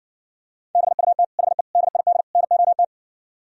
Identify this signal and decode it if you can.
Morse code, heard as a keyed tone, about 64 wpm, 700 Hz. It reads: BUT HE DID NOT